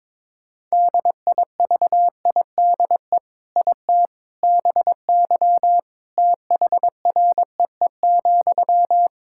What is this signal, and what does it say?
Morse code, heard as a keyed tone, about 21 wpm, 700 Hz. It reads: DIVIDE IT BY THREE,
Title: Sentences from Top 1600 Words 22wpm